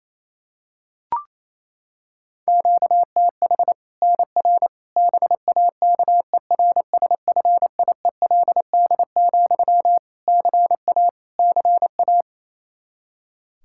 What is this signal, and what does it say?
Morse code, heard as a keyed tone, about 28 wpm, 700 Hz. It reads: QTH NR BAKERSFIELD, CA CA
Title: Practice QSO - Exchange 1 - Beginning - 28wpm